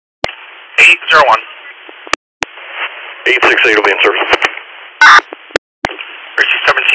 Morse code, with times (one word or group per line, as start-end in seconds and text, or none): none